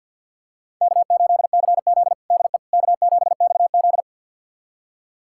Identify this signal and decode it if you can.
Morse code, heard as a keyed tone, about 52 wpm, 700 Hz. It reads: K8CZ DE K7XZ
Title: USA Callsign DE Callsign - 4 Characters - 50wpm